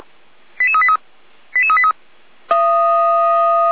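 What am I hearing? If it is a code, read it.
Five-tone selective call (ZVEI), heard as quick runs of five tones, tones of about 80 ms. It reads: 88383 88383
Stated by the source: ZVEI five-tone call received by SDR, from an archive run by Feuerwehr Orlamünde.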